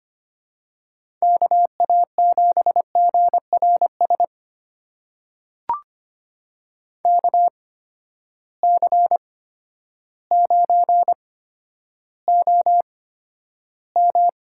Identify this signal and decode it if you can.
Morse code, heard as a keyed tone, about 25 wpm, spaced as for 11 wpm, 700 Hz. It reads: KA7GRS KC9OM
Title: USA Call Signs - 6 Characters - 8x Character Spacing - 25wpm